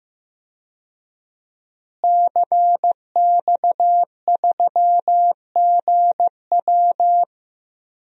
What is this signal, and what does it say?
Morse code, heard as a keyed tone, about 15 wpm, 700 Hz. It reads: CX3GW
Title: DXCC-100 Call Signs - 5 Characters - Repeated 2x - 15wpm